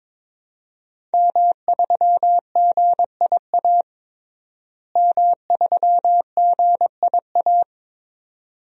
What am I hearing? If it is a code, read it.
Morse code, heard as a keyed tone, about 22 wpm, 700 Hz. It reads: M3GIA M3GIA
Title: DXCC-100 Call Signs - 5 Characters - Repeated 2x - 22wpm